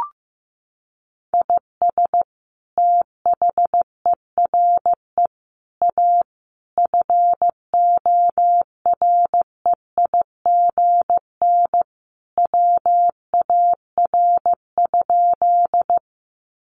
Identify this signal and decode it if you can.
Morse code, heard as a keyed tone, about 15 wpm, 700 Hz. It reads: IS THERE A FOREIGN WAR?
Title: Sentences from Top 1400 Words 15wpm